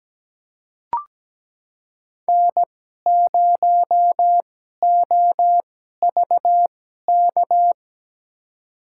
Morse code, heard as a keyed tone, about 17 wpm, spaced as for 12 wpm, 700 Hz. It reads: N0OVK
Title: USA Call Signs - 5 Characters - 2x Character Spacing - 17wpm